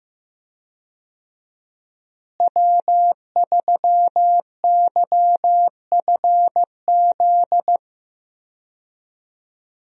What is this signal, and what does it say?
Morse code, heard as a keyed tone, about 15 wpm, 700 Hz. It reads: W3YFZ
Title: USA Call Signs - 5 Characters - Repeated 3x - 15wpm